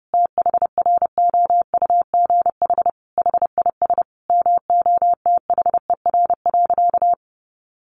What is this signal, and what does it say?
Morse code, heard as a keyed tone, about 30 wpm, 700 Hz. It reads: THROUGH HIS MOTHER.
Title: Sentences from Top 1200 Words 30wpm